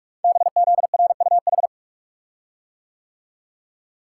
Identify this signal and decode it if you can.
Morse code, heard as a keyed tone, about 44 wpm, 700 Hz. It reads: BZRUH
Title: Code Groups - 5 Letters - 45wpm